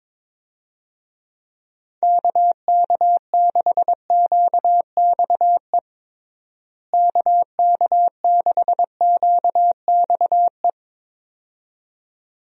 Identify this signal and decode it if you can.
Morse code, heard as a keyed tone, about 22 wpm, 700 Hz. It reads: KK6QXE KK6QXE
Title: USA Call Signs - 6 Characters - Repeated 2x - 22wpm